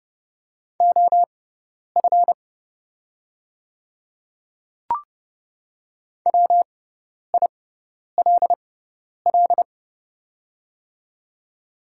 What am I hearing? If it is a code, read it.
Morse code, heard as a keyed tone, about 30 wpm, spaced as for 10 wpm, 690 Hz. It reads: OF WILL
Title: 6x Character Spacing - Top 100 Words - 30wpm